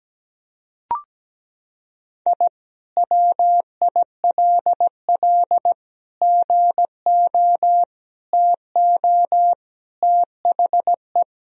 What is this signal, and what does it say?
Morse code, heard as a keyed tone, about 17 wpm, 700 Hz. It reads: I WILL GO TO THE
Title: Sentences from Top 1000 Words - Review Part 1 - 17wpm